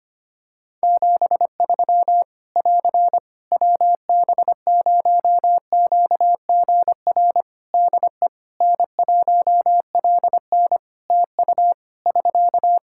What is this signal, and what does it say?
Morse code, heard as a keyed tone, about 25 wpm, 700 Hz. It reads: 73 <AR> WB0QGR DE N1LN TU <SK>
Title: Practice QSO - Exchange 4 - Ending - 25wpm